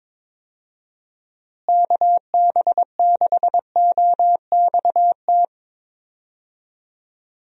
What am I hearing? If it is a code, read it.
Morse code, heard as a keyed tone, about 21 wpm, 700 Hz. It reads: KB6OXT
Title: USA Call Signs - 6 Characters - Repeated 2x - 22wpm